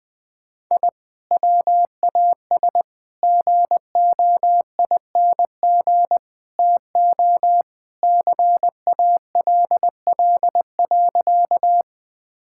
Morse code, heard as a keyed tone, about 20 wpm, 700 Hz. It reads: I WAS GOING TO CALL.
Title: Sentences with 4-Word Phrases - Spoken Beginning - 20wpm